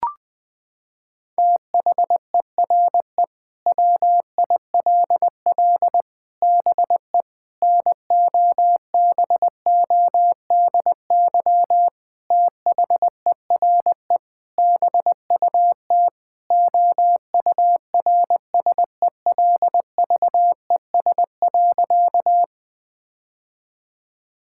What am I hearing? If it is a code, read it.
Morse code, heard as a keyed tone, about 20 wpm, 700 Hz. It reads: THERE WILL BE NOBODY THERE BUT OURSELVES.
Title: Sentences from Top 1600 Words 20wpm